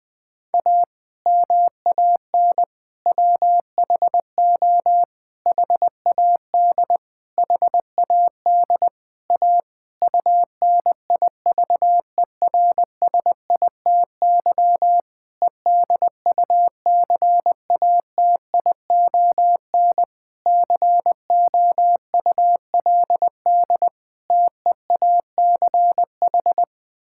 Morse code, heard as a keyed tone, about 20 wpm, 700 Hz. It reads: A MAN WHO HAD HAD A UNIVERSITY EDUCATION COULD TEACH
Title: Sentences from Top 1000 Words - Review Part 6 - 20wpm